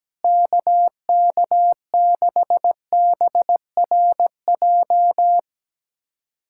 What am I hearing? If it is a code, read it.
Morse code, heard as a keyed tone, about 17 wpm, 700 Hz. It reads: KK6BRJ
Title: USA Callsign DE Callsign - 6 Characters - 17wpm